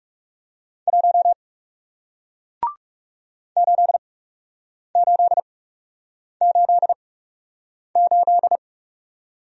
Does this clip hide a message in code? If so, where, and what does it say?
Yes: Morse code from 0.9 to 8.6 s, heard as a keyed tone, about 41 wpm, spaced as for 8 wpm, 700 Hz. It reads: 1 8888